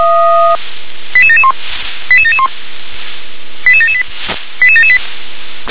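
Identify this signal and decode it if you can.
Five-tone selective call (ZVEI), heard as quick runs of five tones, tones of about 70 ms. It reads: 88771 88771 88778 88778
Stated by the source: Received on an SDR; ZVEI five-tone call 88951.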